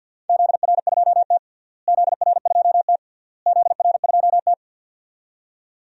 Morse code, heard as a keyed tone, about 51 wpm, 700 Hz. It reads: ZR2T ZR2T ZR2T
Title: DXCC-100 Call Signs - 4 Characters - Repeated 3x - 50wpm